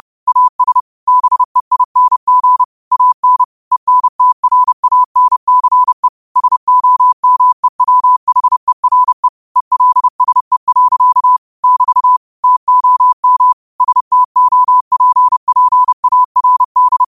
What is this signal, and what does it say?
Morse code, heard as a keyed tone, about 30 wpm, 1000 Hz. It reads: AS BEING AN ENTRANCE SOMEWHERE ELSE. <BT> TOM STOPPARD
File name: qotdfiles / 30wpm